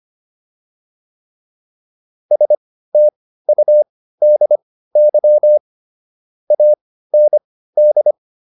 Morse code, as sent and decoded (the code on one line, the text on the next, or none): ... - ..- -.. -.-- / .- -. -..
STUDY AND